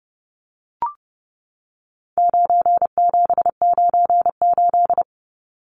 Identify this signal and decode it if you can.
Morse code, heard as a keyed tone, about 29 wpm, 700 Hz. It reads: 9798
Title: Numbers - 4-Digit 30wpm